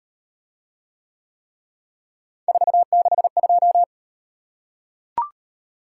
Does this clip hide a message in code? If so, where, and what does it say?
Yes: Morse code from 2.5 to 3.8 s, heard as a keyed tone, about 36 wpm, 700 Hz. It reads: 462